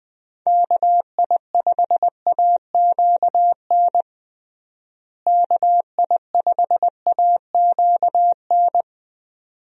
Morse code, heard as a keyed tone, about 21 wpm, 700 Hz. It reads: KI5AQN KI5AQN
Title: USA Call Signs - 6 Characters - Repeated 2x - 20wpm